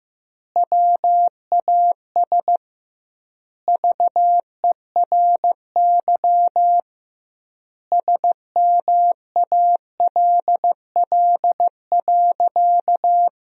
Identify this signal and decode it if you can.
Morse code, heard as a keyed tone, about 16 wpm, 700 Hz. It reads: WAS VERY SMALL.
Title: 2x Word-Spacing - Sentences from Top 400 Words - 15wpm